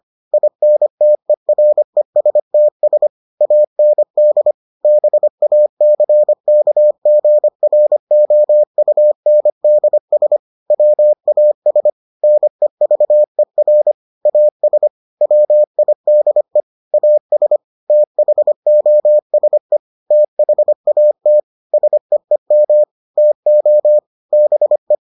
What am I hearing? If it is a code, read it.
Morse code, heard as a keyed tone, about 25 wpm, 600 Hz. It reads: INTERESTS AND BACKGROUNDS WAS NEVER AS WIDE AS THOSE THAT SEEM TO BE